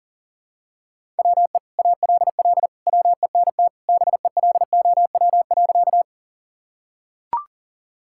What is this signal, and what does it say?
Morse code, heard as a keyed tone, about 40 wpm, 700 Hz. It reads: WE ALL WENT BELOW.
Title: Sentences from Top 1000 Words - Review Part 5 - 40wpm